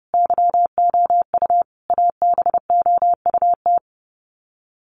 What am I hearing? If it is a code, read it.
Morse code, heard as a keyed tone, about 29 wpm, 700 Hz. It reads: YOU ABOUT
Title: Sets of 5 Words Encore2 30wpm